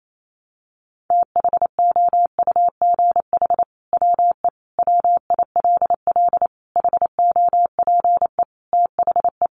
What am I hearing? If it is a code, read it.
Morse code, heard as a keyed tone, about 28 wpm, 700 Hz. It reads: THOUGH WE WILL HOPE THE
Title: Sentences from Top 1000 Words 28wpm